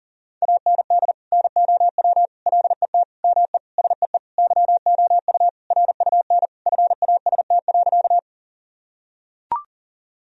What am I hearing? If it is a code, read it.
Morse code, heard as a keyed tone, about 40 wpm, 700 Hz. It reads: AND NOW LET ME SEE YOU RUN FAST.
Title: Sentences from Top 1000 Words - Review Part 2 - 40wpm